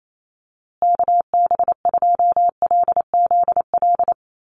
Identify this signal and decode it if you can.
Morse code, heard as a keyed tone, about 28 wpm, 700 Hz. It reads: KB2LZL
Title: USA Call Signs Repeated 3x 28wpm